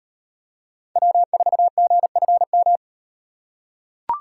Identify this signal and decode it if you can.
Morse code, heard as a keyed tone, about 38 wpm, 700 Hz. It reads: W4GFM